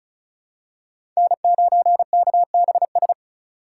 Morse code, heard as a keyed tone, about 34 wpm, 700 Hz. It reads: N9KBS
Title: USA Call Signs - 5 Characters - 4x Character Spacing - 35wpm